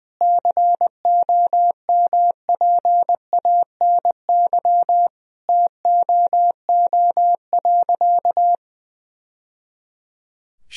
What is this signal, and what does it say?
Morse code, heard as a keyed tone, about 20 wpm, 700 Hz. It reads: COMPANY TOO.
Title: Sentences from Top 1000 Words - Review Part 3 - 20wpm